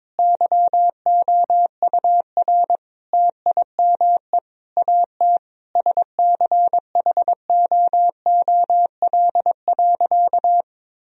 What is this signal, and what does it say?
Morse code, heard as a keyed tone, about 22 wpm, 700 Hz. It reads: YOUR TIME AT SCHOOL.
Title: Sentences from Top 1000 Words - Review Part 1 - 22wpm